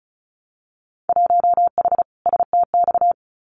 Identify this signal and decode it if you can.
Morse code, heard as a keyed tone, about 36 wpm, 700 Hz. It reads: 1H STX
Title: Winter Field Day Exchanges Encore 35wpm